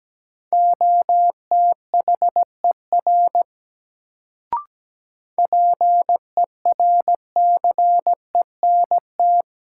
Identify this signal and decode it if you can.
Morse code, heard as a keyed tone, about 17 wpm, 700 Hz. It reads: OTHER PERCENT
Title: Sets of 2 Words - New Word Drill - 17wpm